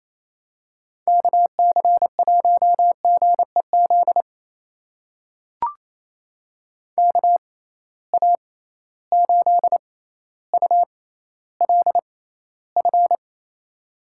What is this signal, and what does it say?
Morse code, heard as a keyed tone, about 28 wpm, spaced as for 14 wpm, 700 Hz. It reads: KC1GEZ KA8ULF
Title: USA Call Signs - 6 Characters - 6x Character Spacing - 28wpm